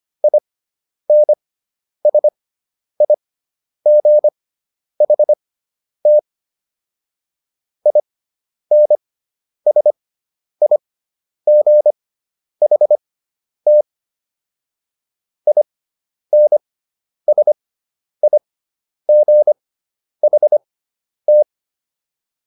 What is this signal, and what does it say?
Morse code, heard as a keyed tone, about 25 wpm, spaced as for 10 wpm, 600 Hz. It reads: INSIGHT INSIGHT INSIGHT